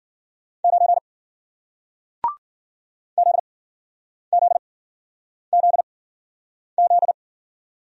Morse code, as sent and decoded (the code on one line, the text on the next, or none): ----. / --.. --.. --.. --..
9 ZZZZ